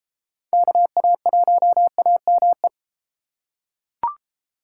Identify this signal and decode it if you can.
Morse code, heard as a keyed tone, about 31 wpm, 700 Hz. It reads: KA1AME